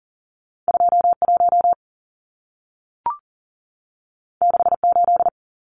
Morse code, heard as a keyed tone, about 40 wpm, 700 Hz. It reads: 21 68
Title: Numbers - 2-Digit 40wpm